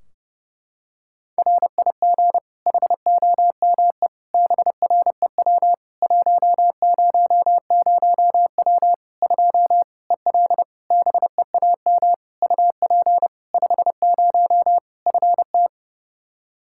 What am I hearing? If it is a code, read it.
Morse code, heard as a keyed tone, about 30 wpm, 700 Hz. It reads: RIG HOME BREW 100W 2 EL BEAM UP 50 FT
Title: Practice QSO - Exchange 2 - Info - 30wpm